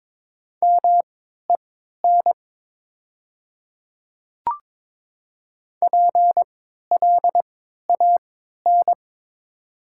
Morse code, heard as a keyed tone, about 22 wpm, spaced as for 13 wpm, 700 Hz. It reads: MEN PLAN
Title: 3x Character Spacing - Top 500 Words - 22wpm